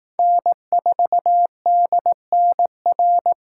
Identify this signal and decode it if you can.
Morse code, heard as a keyed tone, about 18 wpm, 700 Hz. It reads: N4DNR
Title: Speed Racing - USA 1x3 Call Signs 15wpm